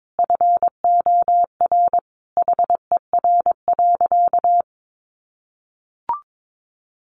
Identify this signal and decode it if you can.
Morse code, heard as a keyed tone, about 22 wpm, 700 Hz. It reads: FOR HER.
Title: Sentences from Top 1300 Words 22wpm